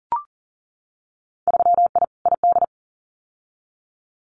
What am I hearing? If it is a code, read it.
Morse code, heard as a keyed tone, about 42 wpm, 700 Hz. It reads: 3I ID
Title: Winter Field Day Exchanges 40wpm